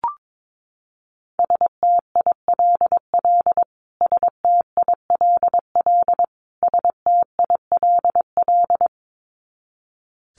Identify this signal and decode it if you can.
Morse code, heard as a keyed tone, about 22 wpm, 700 Hz. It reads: STILL STILL STILL
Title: Top 300 Words Repeated 22wpm